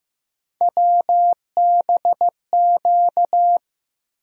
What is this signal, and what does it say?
Morse code, heard as a keyed tone, about 15 wpm, 700 Hz. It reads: WBQ
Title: USA Call Sign Postfixes - 3 Letters 15wpm